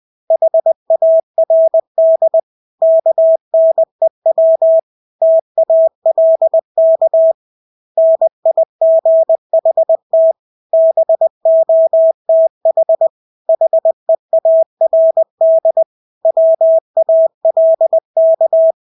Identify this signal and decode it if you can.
Morse code, heard as a keyed tone, about 20 wpm, 650 Hz. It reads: HARD KNEW TALK NIGHT BOTH HEARD WALK
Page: Engelske ord 20 wpm